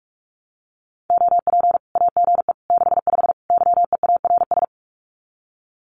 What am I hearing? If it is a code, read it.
Morse code, heard as a keyed tone, about 44 wpm, 700 Hz. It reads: OP AGE 65 YEARS
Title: Common QSO Phrases 45wpm